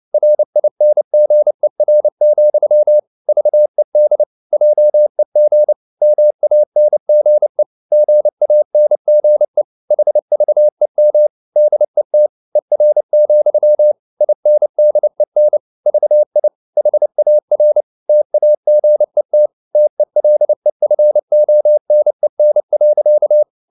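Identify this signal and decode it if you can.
Morse code, heard as a keyed tone, about 29 wpm, 600 Hz. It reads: RINGER, VED JEG MANGE GANGE HVEM DET ER, INDEN VI HAR TAGET TELEFONEN.
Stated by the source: Fast (QRQ) Morse at 29 wpm, from CW med Gnister.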